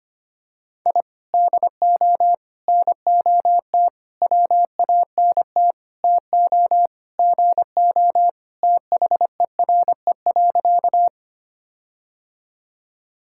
Morse code, heard as a keyed tone, about 25 wpm, 700 Hz. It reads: I DO NOT WANT TO GO THERE.
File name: practice / 25wpm / Sentences with 4-Word Phrases - 25wpm